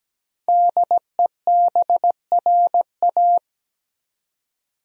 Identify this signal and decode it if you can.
Morse code, heard as a keyed tone, about 17 wpm, 700 Hz. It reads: DEBRA
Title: USA Common Names - 17wpm